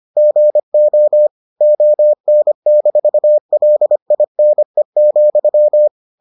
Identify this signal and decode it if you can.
Morse code, heard as a keyed tone, about 25 wpm, 600 Hz. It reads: GO ON-LINE,